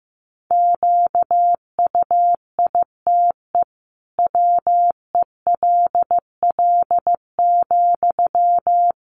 Morse code, heard as a keyed tone, about 15 wpm, 700 Hz. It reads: QUITE WELL,
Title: Sentences from Top 700 Words 15wpm